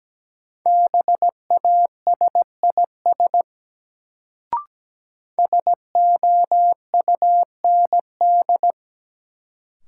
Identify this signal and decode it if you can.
Morse code, heard as a keyed tone, about 17 wpm, 700 Hz. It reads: BASIS SOUND
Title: Sets of 3 Words - New Word Drill - 17wpm